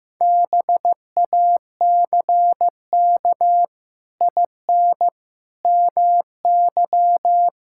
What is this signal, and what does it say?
Morse code, heard as a keyed tone, about 14 wpm, 700 Hz. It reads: BACK IN MY
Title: Pangram Challenge v2 - 15wpm